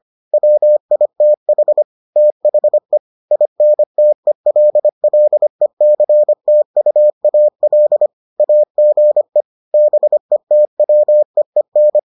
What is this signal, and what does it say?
Morse code, heard as a keyed tone, about 25 wpm, 600 Hz. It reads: WITH THE INTELLECTUAL AGE BETWEEN